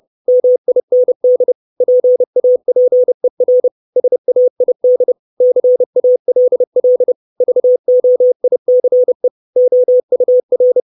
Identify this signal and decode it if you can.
Morse code, heard as a keyed tone, about 30 wpm, 500 Hz. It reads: MIND PAPER SAID CALL VOICE OUR
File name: 632 Words 30WPM